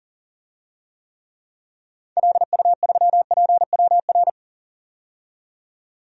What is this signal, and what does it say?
Morse code, heard as a keyed tone, about 40 wpm, 700 Hz. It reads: LU3PWR
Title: DXCC-100 Call Signs - 6 Characters - 40wpm